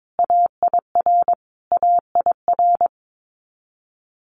Morse code, heard as a keyed tone, about 22 wpm, 700 Hz. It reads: AIR AIR
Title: Multi-letter Phonemes Repeated 3x 22wpm